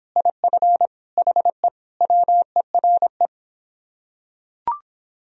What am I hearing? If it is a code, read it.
Morse code, heard as a keyed tone, about 25 wpm, 700 Hz. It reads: IF HE WERE